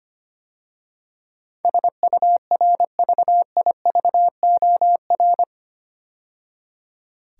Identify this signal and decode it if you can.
Morse code, heard as a keyed tone, about 25 wpm, 700 Hz. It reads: SURVIVOR
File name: 8 Letter Words - First 3 Letters Given - 25wpm